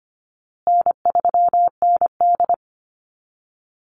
Morse code, heard as a keyed tone, about 25 wpm, 700 Hz. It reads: N3ND
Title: Easy Call Signs Repeated 3x - 25wpm